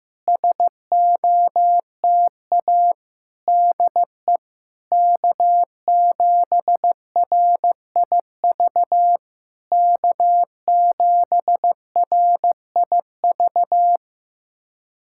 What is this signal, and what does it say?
Morse code, heard as a keyed tone, about 15 wpm, 690 Hz. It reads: SOTA DE K7RIV K7RIV
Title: SOTA QSO Practice 15wpm